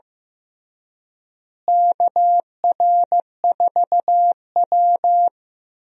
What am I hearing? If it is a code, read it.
Morse code, heard as a keyed tone, about 16 wpm, 700 Hz. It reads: KR4W
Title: USA Call Signs - 4 Characters - Repeated 3x - 15wpm